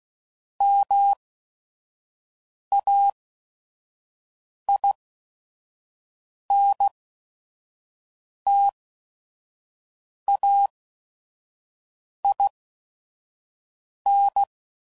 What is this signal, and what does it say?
Morse code, heard as a keyed tone, about 16 wpm, spaced as for 5 wpm, 800 Hz. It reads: MAINTAIN
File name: CW Code Groups @5WPM / Technical Subjects / XC